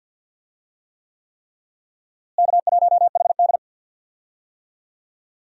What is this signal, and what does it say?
Morse code, heard as a keyed tone, about 46 wpm, 700 Hz. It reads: K1HD
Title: USA Call Signs - 4 Characters - Repeated 3x - 50wpm